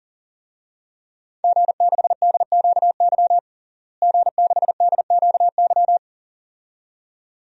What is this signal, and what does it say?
Morse code, heard as a keyed tone, about 40 wpm, 700 Hz. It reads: G6DQY G6DQY
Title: DXCC-100 Call Signs - 5 Characters - Repeated 2x - 40wpm